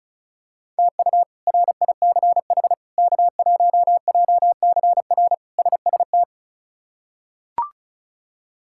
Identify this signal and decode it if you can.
Morse code, heard as a keyed tone, about 36 wpm, 700 Hz. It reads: TU RICH K1JCR SST
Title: K1USN SST QSO Practice - 35wpm